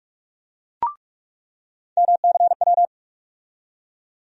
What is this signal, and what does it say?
Morse code, heard as a keyed tone, about 44 wpm, 700 Hz. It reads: MCW
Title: USA Call Sign Postfixes - 3 Letters 45wpm